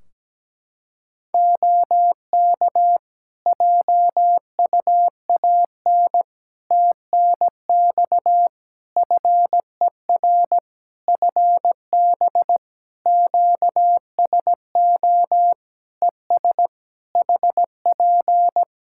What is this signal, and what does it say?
Morse code, heard as a keyed tone, about 17 wpm, 700 Hz. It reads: OK JUAN TNX FER FB QSO ES HP
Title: K1USN SST QSO Practice 17wpm